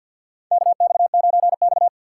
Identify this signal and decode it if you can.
Morse code, heard as a keyed tone, about 51 wpm, 700 Hz. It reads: KX9X
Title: USA Call Signs - 4 Characters - 6x Character Spacing - 50wpm